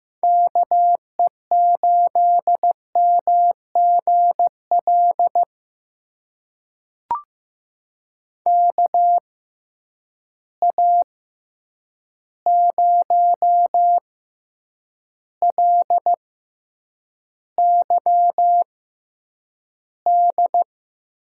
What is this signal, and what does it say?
Morse code, heard as a keyed tone, about 15 wpm, spaced as for 8 wpm, 700 Hz. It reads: KE8MGL KA0LYD
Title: USA Call Signs - 6 Characters - 6x Character Spacing - 15wpm